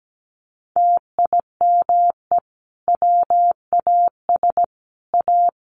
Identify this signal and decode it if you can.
Morse code, heard as a keyed tone, about 17 wpm, 700 Hz. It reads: TIME WAS A
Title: A Sound Of Thunder 17wpm